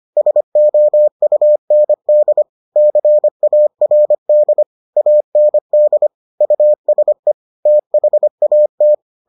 Morse code, heard as a keyed tone, about 25 wpm, 600 Hz. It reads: SOUND CARD AND USE THAT